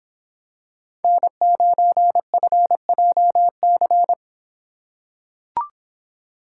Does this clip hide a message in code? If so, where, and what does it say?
Yes: Morse code from 1.0 to 4.1 s, heard as a keyed tone, about 26 wpm, 700 Hz. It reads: N9FJC